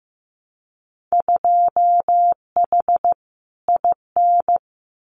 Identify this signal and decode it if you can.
Morse code, heard as a keyed tone, about 15 wpm, 700 Hz. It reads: 2H IN
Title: Winter Field Day Exchanges Encore 15wpm